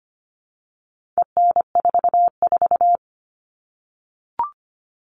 Morse code, heard as a keyed tone, about 25 wpm, 700 Hz. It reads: EN44
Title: All Cut Numbers - 4-Digit 25wpm